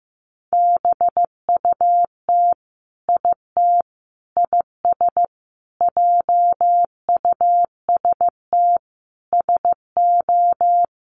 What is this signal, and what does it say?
Morse code, heard as a keyed tone, about 14 wpm, 700 Hz. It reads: BUT IT IS JUST SO
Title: Sentences from Top 200 Words 15wpm